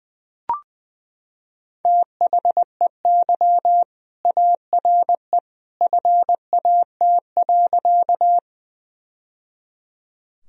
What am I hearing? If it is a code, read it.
Morse code, heard as a keyed tone, about 20 wpm, 700 Hz. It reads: THEY ARE FAT.
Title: Sentences from Top 1000 Words - Review Part 4 - 20wpm